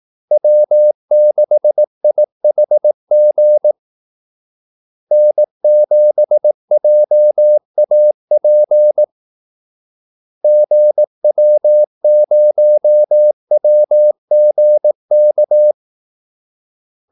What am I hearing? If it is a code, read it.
Morse code, heard as a keyed tone, about 18 wpm, 600 Hz. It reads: W6IHG N7JAP GW0WGK
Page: Kaldesignaler 18 wpm